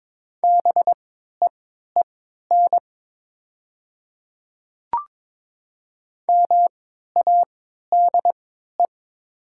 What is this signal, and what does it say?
Morse code, heard as a keyed tone, about 22 wpm, spaced as for 12 wpm, 700 Hz. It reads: BEEN MADE
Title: 3x Character Spacing - Top 300 Words - 22wpm